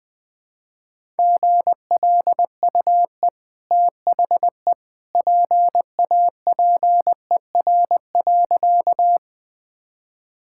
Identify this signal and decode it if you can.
Morse code, heard as a keyed tone, about 20 wpm, 700 Hz. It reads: GLUE THE PAPER.
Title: Easy Sentences with Hints - 20wpm